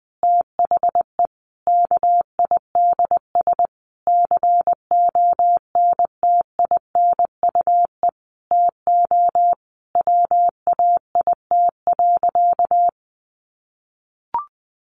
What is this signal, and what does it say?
Morse code, heard as a keyed tone, about 20 wpm, 700 Hz. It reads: THE KIDS CONTINUE TO WAIT.
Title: Easy Phrases 20wpm